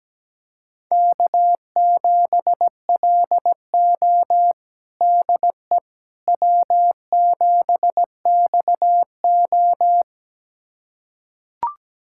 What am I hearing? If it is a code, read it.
Morse code, heard as a keyed tone, about 17 wpm, 700 Hz. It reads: K7LO DE W7XO
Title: USA Callsign DE Callsign - 4 Characters - 17wpm